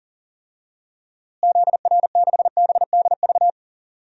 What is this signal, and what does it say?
Morse code, heard as a keyed tone, about 37 wpm, 700 Hz. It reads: ZR6BDV
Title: DXCC-100 Call Signs - 6 Characters - Repeated 2x - 40wpm